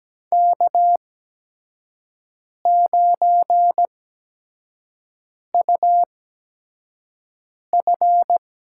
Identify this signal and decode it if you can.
Morse code, heard as a keyed tone, about 17 wpm, spaced as for 5 wpm, 700 Hz. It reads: K9UF
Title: USA Call Signs - 4 Characters - 8x Character Spacing - 17wpm